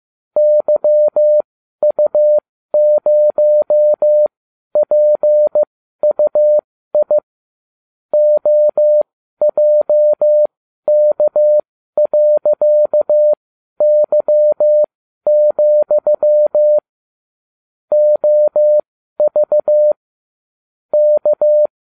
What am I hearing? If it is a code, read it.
Morse code, heard as a keyed tone, about 15 wpm, spaced as for 12 wpm, 600 Hz. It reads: YU0PUI OJK.Y, OV K